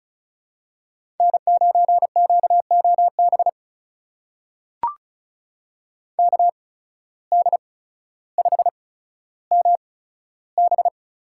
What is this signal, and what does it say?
Morse code, heard as a keyed tone, about 36 wpm, spaced as for 15 wpm, 700 Hz. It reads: N9QOB KD5MB
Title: USA Call Signs - 5 Characters - 8x Character Spacing - 35wpm